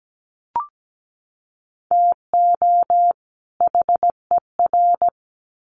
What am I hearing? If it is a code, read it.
Morse code, heard as a keyed tone, about 17 wpm, 700 Hz. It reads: TO HER
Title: Sets of 2 Words 17wpm